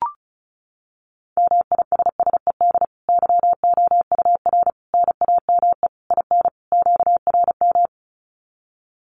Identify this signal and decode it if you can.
Morse code, heard as a keyed tone, about 34 wpm, 700 Hz. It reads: MISSED YOUR NAME IN QRM
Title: Common QSO Phrases 35wpm